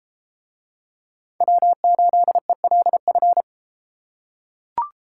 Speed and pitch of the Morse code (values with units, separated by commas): 31 wpm, 700 Hz